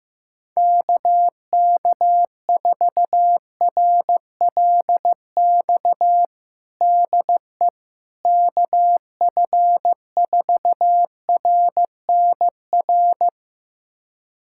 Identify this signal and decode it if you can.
Morse code, heard as a keyed tone, about 16 wpm, 700 Hz. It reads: KK4RLX DE KF4RNR
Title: USA Callsign DE Callsign - 6 Characters - 15wpm